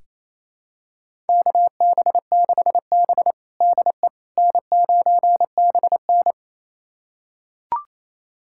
Morse code, heard as a keyed tone, about 28 wpm, 700 Hz. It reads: KB6B DE N9BN
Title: USA Callsign DE Callsign - 4 Characters - 28wpm